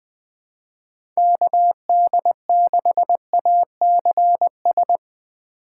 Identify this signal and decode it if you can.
Morse code, heard as a keyed tone, about 20 wpm, 700 Hz. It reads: KD6ACS
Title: USA Call Signs - 6 Characters - 8x Character Spacing - 20wpm